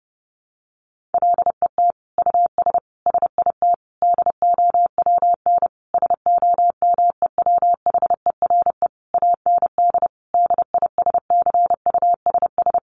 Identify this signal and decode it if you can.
Morse code, heard as a keyed tone, about 28 wpm, 700 Hz. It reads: LET US SIT DOWN SOMEWHERE AND DISCUSS
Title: Sentences from Top 1400 Words 30wpm